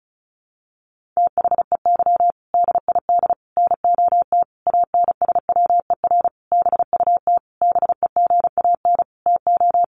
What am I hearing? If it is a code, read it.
Morse code, heard as a keyed tone, about 34 wpm, 700 Hz. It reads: THEY DID NOT ANSWER BUT BEGAN TO
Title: Sentences from Top 600 Words 35wpm